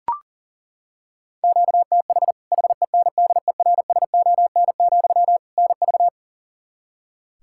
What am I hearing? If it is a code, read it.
Morse code, heard as a keyed tone, about 40 wpm, 700 Hz. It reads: QTH HENDERSON, NV
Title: Practice QSO - Exchange 1 - Beginning - 40wpm